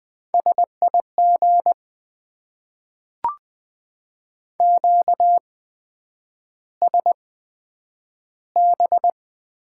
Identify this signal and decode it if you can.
Morse code, heard as a keyed tone, about 20 wpm, spaced as for 9 wpm, 700 Hz. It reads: SIG QSB
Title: Top 90 QSO Elements - 8x Character Spacing - 20wpm